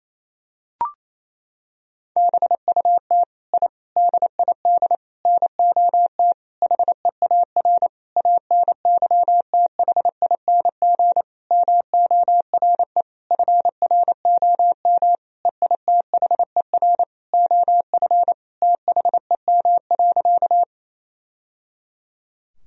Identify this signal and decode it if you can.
Morse code, heard as a keyed tone, about 28 wpm, 700 Hz. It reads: BUT I DID NOT HEAR ANYTHING MORE FROM EITHER OF THEM.
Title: Sentences from Top 1000 Words - Review Part 4 - 28wpm